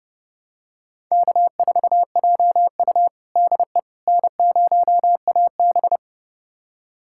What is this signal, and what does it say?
Morse code, heard as a keyed tone, about 30 wpm, 700 Hz. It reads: K4JU DE N0AB
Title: USA Callsign DE Callsign - 4 Characters - 30wpm